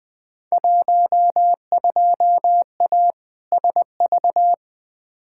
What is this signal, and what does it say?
Morse code, heard as a keyed tone, about 20 wpm, 700 Hz. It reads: 12A SV
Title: Field Day Exchanges Encore - 20wpm